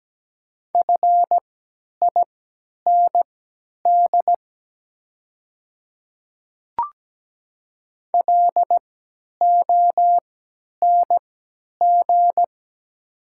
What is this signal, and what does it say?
Morse code, heard as a keyed tone, about 17 wpm, spaced as for 10 wpm, 700 Hz. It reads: FIND LONG
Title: 3x Character Spacing - Top 100 Words - 17wpm